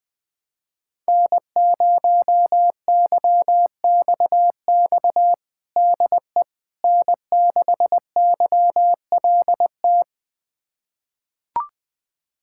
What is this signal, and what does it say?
Morse code, heard as a keyed tone, about 20 wpm, 700 Hz. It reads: N0YXX DE N6YLT
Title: USA Callsign DE Callsign - 5 Characters - 20wpm